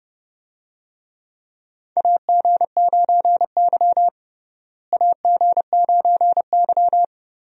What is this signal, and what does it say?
Morse code, heard as a keyed tone, about 32 wpm, 700 Hz. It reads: AG9Y AG9Y
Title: USA Call Signs - 4 Characters - Repeated 2x - 30wpm